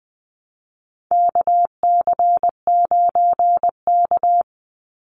Morse code, heard as a keyed tone, about 20 wpm, 700 Hz. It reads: KC9K
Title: Call Signs - 20wpm